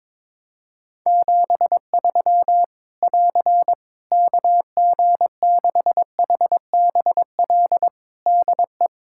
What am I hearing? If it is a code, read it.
Morse code, heard as a keyed tone, about 22 wpm, 700 Hz. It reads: 73 <AR> KG6HBL DE
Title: Practice QSO - Exchange 4 - Ending - 22wpm